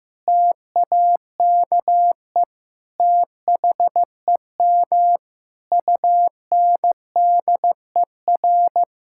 Morse code, heard as a keyed tone, about 15 wpm, 700 Hz. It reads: TAKE THEM UNDER
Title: Sentences from Top 1600 Words 15wpm